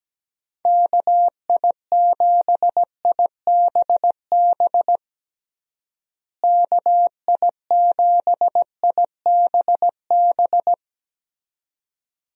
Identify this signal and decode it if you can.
Morse code, heard as a keyed tone, about 17 wpm, 700 Hz. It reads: KI7IBB KI7IBB
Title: USA Call Signs - 6 Characters - Repeated 2x - 17wpm